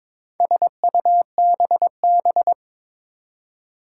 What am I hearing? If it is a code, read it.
Morse code, heard as a keyed tone, about 22 wpm, 700 Hz. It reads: SUBB
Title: Four Challenging Letter-Number Combinations 22wpm